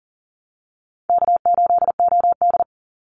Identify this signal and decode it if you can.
Morse code, heard as a keyed tone, about 40 wpm, 700 Hz. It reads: K8OD
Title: Easy Call Signs Repeated 3x - 40wpm